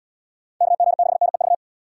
Morse code, heard as a keyed tone, about 70 wpm, 700 Hz. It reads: KG6NEV